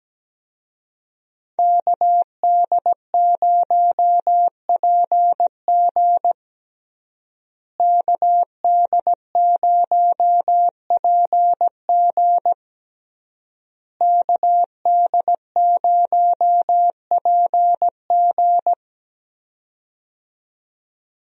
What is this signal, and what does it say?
Morse code, heard as a keyed tone, about 17 wpm, 700 Hz. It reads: KD0PG KD0PG KD0PG
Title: USA Call Signs - 5 Characters - Repeated 3x - 17wpm